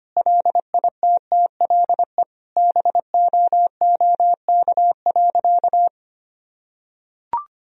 Morse code, heard as a keyed tone, about 25 wpm, 700 Hz. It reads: LITTLE BOOK.
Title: Sentences from Top 1000 Words - Review Part 2 - 25wpm